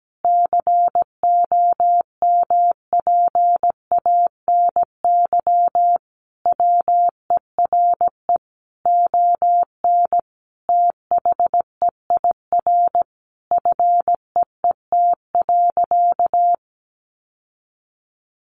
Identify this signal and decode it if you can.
Morse code, heard as a keyed tone, about 17 wpm, 700 Hz. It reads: COMPANY WERE ON THEIR FEET.
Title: Sentences from Top 1000 Words 17wpm